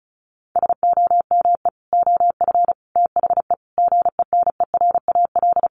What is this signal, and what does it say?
Morse code, heard as a keyed tone, about 34 wpm, 700 Hz. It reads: SOME OF THE GENERAL
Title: Sentences from Top 800 Words 35wpm